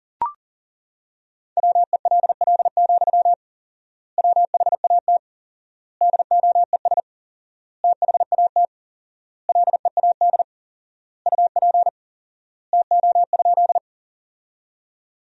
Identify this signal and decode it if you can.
Morse code, heard as a keyed tone, about 40 wpm, 700 Hz. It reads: WELL, WHAT DOES THAT LEAD UP TO?
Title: 4x Word-Spacing - Sentences from Top 500 Words - 40wpm